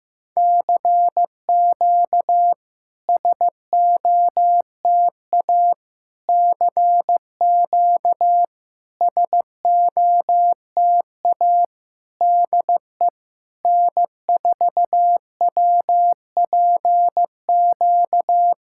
Morse code, heard as a keyed tone, about 15 wpm, 700 Hz. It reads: CQ SOTA CQ SOTA DE N4WPQ
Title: SOTA QSO Practice 15wpm